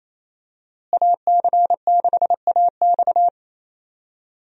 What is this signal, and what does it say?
Morse code, heard as a keyed tone, about 27 wpm, 700 Hz. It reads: AC6AX
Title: USA Call Signs - 5 Characters - Repeated 3x - 28wpm